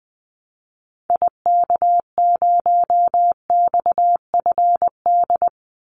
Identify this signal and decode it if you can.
Morse code, heard as a keyed tone, about 20 wpm, 700 Hz. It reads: IK0XFD
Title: Call Signs - 20wpm